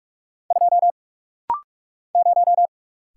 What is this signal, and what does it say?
Morse code, heard as a keyed tone, about 46 wpm, 700 Hz. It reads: 2 0
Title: Single Letter-Number - Rapid-Fire 45wpm